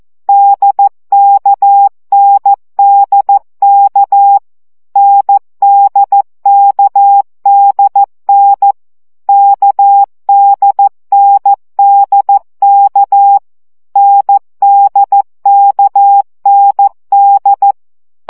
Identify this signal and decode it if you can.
Morse code, heard as a keyed tone, about 14 wpm, 800 Hz. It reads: DKNDK NDKDN KDNDK NDKND
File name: CW gyakorlas n d k 12 baud